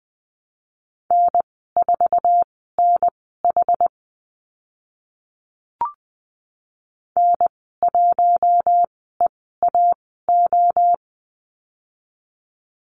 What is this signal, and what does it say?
Morse code, heard as a keyed tone, about 20 wpm, spaced as for 14 wpm, 700 Hz. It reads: N4NH N1EAO
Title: 069 Lesson 12g - Letter d - Call Signs 20wpm